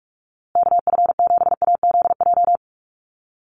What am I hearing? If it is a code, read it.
Morse code, heard as a keyed tone, about 44 wpm, 700 Hz. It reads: KF7AZJ
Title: USA Call Signs Repeated 3x 45wpm